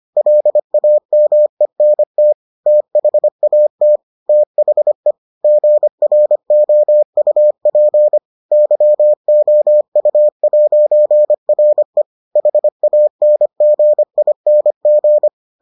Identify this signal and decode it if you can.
Morse code, heard as a keyed tone, about 25 wpm, 600 Hz. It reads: LAMENT THAT THE GROUP YOU'RE HANGING